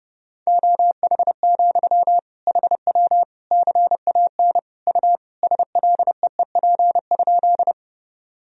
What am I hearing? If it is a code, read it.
Morse code, heard as a keyed tone, about 30 wpm, 700 Hz. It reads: OH, HW CAN U SLEEP?
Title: Sentences from Top 500 Words and Abbreviations 30wpm